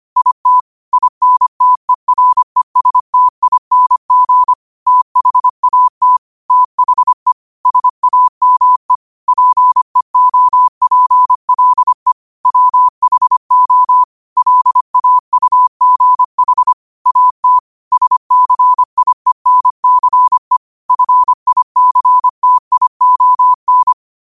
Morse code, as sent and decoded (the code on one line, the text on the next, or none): .. - / .. -. - . .-. ... - .. -. --. / - .... .- - / - .... / ... .- -- . / .--. --- .--. .-.. / .-- .... --- / .-.. .- ..- --. .... / .- - / ... -.-. .. -. -.-. / ..-. .. -.-. - .. --- -.
IT INTERESTING THAT THE SAME PEOPLE WHO LAUGH AT SCIENCE FICTION